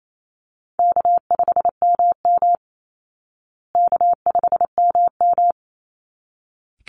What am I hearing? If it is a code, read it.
Morse code, heard as a keyed tone, about 28 wpm, 700 Hz. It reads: K5MM K5MM
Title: Easy Call Signs Repeated 2x - 28wpm